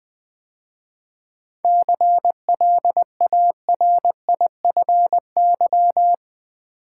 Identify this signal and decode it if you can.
Morse code, heard as a keyed tone, about 21 wpm, 700 Hz. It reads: CLARIFY
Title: 7 Letter Words - First 3 Letters Given - 20wpm